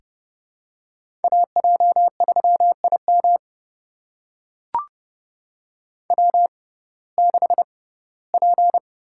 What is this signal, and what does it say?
Morse code, heard as a keyed tone, about 30 wpm, spaced as for 18 wpm, 700 Hz. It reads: AJ3IM W6P
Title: USA Call Signs - 5 Characters - 6x Character Spacing - 30wpm